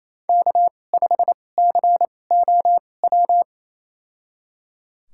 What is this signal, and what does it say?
Morse code, heard as a keyed tone, about 28 wpm, spaced as for 20 wpm, 700 Hz. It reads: K5COW
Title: USA Call Signs - 5 Characters - 2x Character Spacing - 28wpm